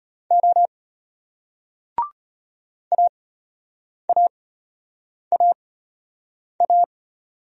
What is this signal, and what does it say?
Morse code, heard as a keyed tone, about 33 wpm, spaced as for 8 wpm, 700 Hz. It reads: O AAAA